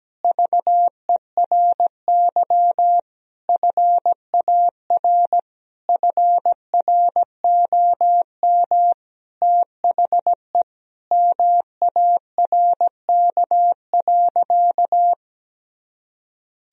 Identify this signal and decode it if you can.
Morse code, heard as a keyed tone, about 17 wpm, 700 Hz. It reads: VERY FAR FROM THE MARK.
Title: Sentences from Top 1000 Words - Review Part 1 - 17wpm